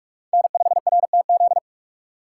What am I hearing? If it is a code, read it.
Morse code, heard as a keyed tone, about 46 wpm, 700 Hz. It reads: N5RTZ
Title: USA Call Signs - 5 Characters - 3x Character Spacing - 45wpm